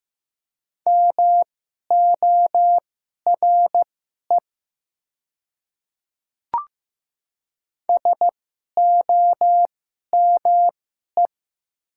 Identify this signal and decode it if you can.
Morse code, heard as a keyed tone, about 15 wpm, spaced as for 11 wpm, 700 Hz. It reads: MORE SOME
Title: 2x Character Spacing - Top 100 Words - 15wpm